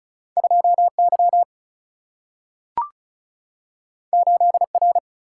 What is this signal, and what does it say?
Morse code, heard as a keyed tone, about 36 wpm, 700 Hz. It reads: 2 Y 8 R
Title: Single-Letter Single-Number 35wpm